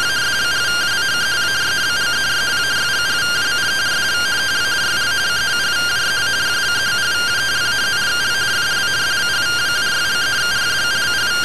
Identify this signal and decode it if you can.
Two-tone teleprinter signal, about 170 Hz shift.